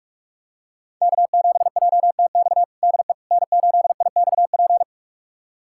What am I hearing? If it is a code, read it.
Morse code, heard as a keyed tone, about 41 wpm, 700 Hz. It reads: K7JTX DE N8IXP
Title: USA Callsign DE Callsign - 5 Characters - 45wpm